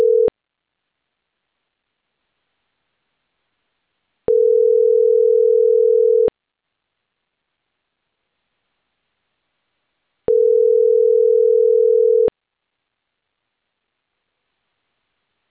Ringback tone (North America), 440 + 480 Hz.